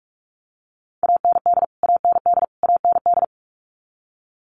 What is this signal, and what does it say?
Morse code, heard as a keyed tone, about 44 wpm, 700 Hz. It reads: AND AND AND
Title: Top 500 Words Repeated 45wpm